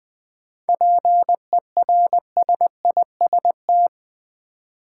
Morse code, heard as a keyed tone, about 20 wpm, 700 Hz. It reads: PERSIST
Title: 7 Letter Words - First 3 Letters Given - 20wpm